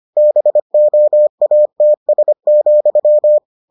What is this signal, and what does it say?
Morse code, heard as a keyed tone, about 25 wpm, 600 Hz. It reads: BOATS,